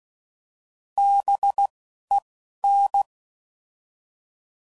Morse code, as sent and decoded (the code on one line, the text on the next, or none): -... / . / -.
B E N